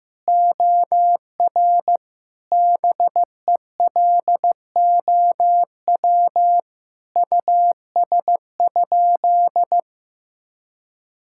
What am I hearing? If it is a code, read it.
Morse code, heard as a keyed tone, about 16 wpm, 700 Hz. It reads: OR BELOW US?
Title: Sentences from Top 1000 Words - Review Part 5 - 15wpm